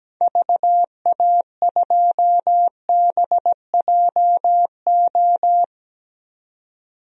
Morse code, heard as a keyed tone, about 17 wpm, 700 Hz. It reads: VA2BJO
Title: DXCC-100 Call Signs - 6 Characters - 17wpm